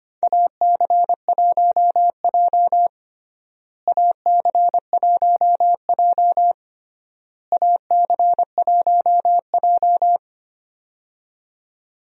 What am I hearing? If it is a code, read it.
Morse code, heard as a keyed tone, about 25 wpm, 700 Hz. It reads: AC1J AC1J AC1J
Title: USA Call Signs - 4 Characters - Repeated 3x - 25wpm